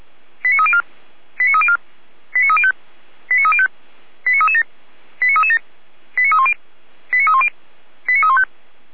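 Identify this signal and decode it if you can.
Five-tone selective call (ZVEI), heard as quick runs of five tones, tones of about 80 ms. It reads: 89304 89304 89305 89305 89337 89337 89310 89310 89315
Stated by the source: Five-tone ZVEI selective call received on an SDR.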